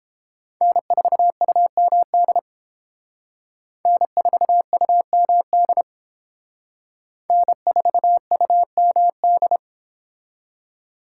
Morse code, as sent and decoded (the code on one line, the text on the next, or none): -. ....- ..- -- -.. / -. ....- ..- -- -.. / -. ....- ..- -- -..
N4UMD N4UMD N4UMD